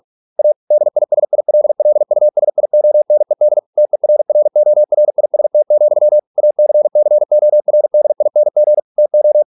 Morse code, heard as a keyed tone, about 46 wpm, 600 Hz. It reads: A DISILLUSIONED TERRORIST, ACCORDING TO